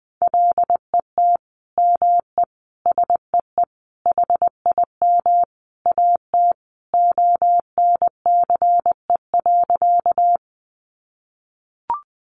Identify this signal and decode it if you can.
Morse code, heard as a keyed tone, about 20 wpm, 700 Hz. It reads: LET ME SEE HIM AT ONCE.
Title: Sentences from Top 300 Words 20wpm